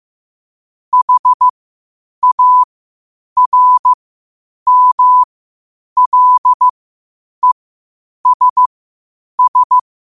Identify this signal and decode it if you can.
Morse code, heard as a keyed tone, about 15 wpm, spaced as for 8 wpm, 1000 Hz. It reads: HARMLESS
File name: qotdfiles / 8wpm